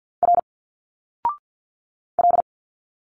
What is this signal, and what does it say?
Morse code, heard as a keyed tone, about 51 wpm, 700 Hz. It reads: R L
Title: Single Letters 50wpm